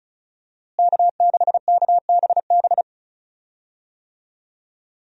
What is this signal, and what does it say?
Morse code, heard as a keyed tone, about 34 wpm, 700 Hz. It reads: K6KBB
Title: USA Call Signs - 5 Characters - Repeated 3x - 35wpm